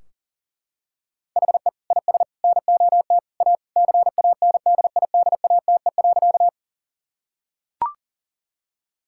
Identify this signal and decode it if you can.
Morse code, heard as a keyed tone, about 40 wpm, 700 Hz. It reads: HE IS NOT A CANDIDATE.
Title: Sentences from Top 1600 Words 40wpm